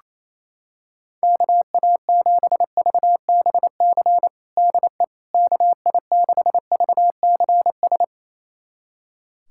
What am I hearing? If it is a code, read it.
Morse code, heard as a keyed tone, about 27 wpm, 700 Hz. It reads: KA7VBC DE KI6VCS
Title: USA Callsign DE Callsign - 6 Characters - 28wpm